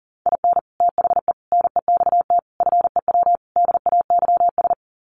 Morse code, heard as a keyed tone, about 40 wpm, 700 Hz. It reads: IN THE NEXT FEW DAYS
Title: Sets of 5 Words Encore2 40wpm